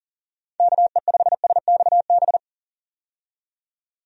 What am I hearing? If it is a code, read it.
Morse code, heard as a keyed tone, about 37 wpm, 700 Hz. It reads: KE5SXB
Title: USA Call Signs - 6 Characters - 40wpm